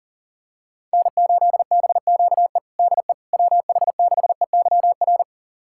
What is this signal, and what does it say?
Morse code, heard as a keyed tone, about 40 wpm, 700 Hz. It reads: N8BQE DE WH6EYR
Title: USA Callsign DE Callsign 40wpm